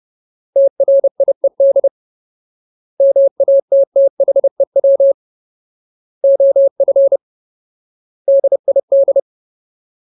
Morse code, heard as a keyed tone, about 30 wpm, 550 Hz. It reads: TRIED MATTHEW OF DID
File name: Ultimate Word List / 30WPM Ultimate